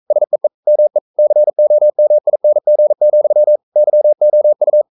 Morse code, heard as a keyed tone, about 42 wpm, 600 Hz. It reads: SEE ME COMING, YOU
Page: Romanen Never af Ken Follet på engelsk.